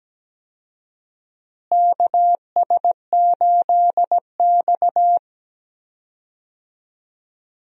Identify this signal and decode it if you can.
Morse code, heard as a keyed tone, about 17 wpm, 700 Hz. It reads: KS8X
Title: USA Call Signs - 4 Characters - Repeated 3x - 17wpm